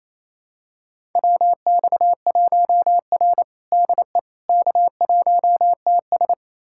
Morse code, heard as a keyed tone, about 27 wpm, 700 Hz. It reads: WX1R DE K1TS
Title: USA Callsign DE Callsign - 4 Characters - 28wpm